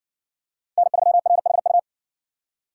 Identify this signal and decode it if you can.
Morse code, heard as a keyed tone, about 60 wpm, 700 Hz. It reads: N3RHU